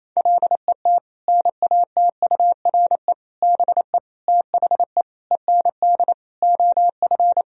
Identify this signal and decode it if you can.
Morse code, heard as a keyed tone, about 28 wpm, 700 Hz. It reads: LET NATURE BE THE END OF
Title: Sentences from Top 1000 Words - Review Part 3 - 28wpm